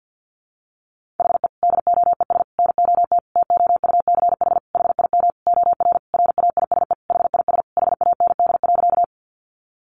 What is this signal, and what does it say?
Morse code, heard as a keyed tone, about 48 wpm, 700 Hz. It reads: HE DOES NOT TOUCH HIM OR RAISE HIS HAND.
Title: Sentences from Top 1400 Words 50wpm